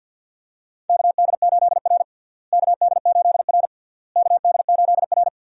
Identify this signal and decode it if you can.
Morse code, heard as a keyed tone, about 49 wpm, 700 Hz. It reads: KD8R KD8R KD8R
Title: USA Call Signs - 4 Characters - Repeated 3x - 50wpm